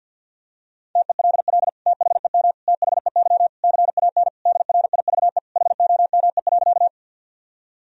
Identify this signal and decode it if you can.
Morse code, heard as a keyed tone, about 50 wpm, 700 Hz. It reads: TELL THEM THEY CAN DRIVE HOME.
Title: Sentences from Top 1000 Words - Review Part 2 - 50wpm